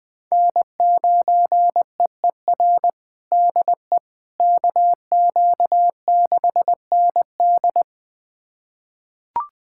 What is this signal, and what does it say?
Morse code, heard as a keyed tone, about 20 wpm, 700 Hz. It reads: N9EER DE KQ6ND
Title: USA Callsign DE Callsign - 5 Characters - 20wpm